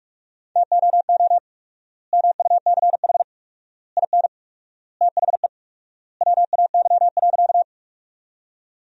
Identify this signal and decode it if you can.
Morse code, heard as a keyed tone, about 46 wpm, 700 Hz. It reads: TOO MUCH IN THE WAY.
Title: 4x Word-Spacing - Sentences from Top 200 Words - 45wpm